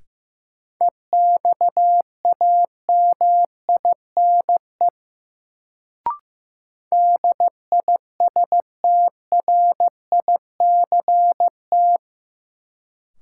Morse code, heard as a keyed tone, about 15 wpm, 700 Hz. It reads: EXAMINE DISTRICT
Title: Sentences from Top 1600 Words - New Word Drill Alt - 15wpm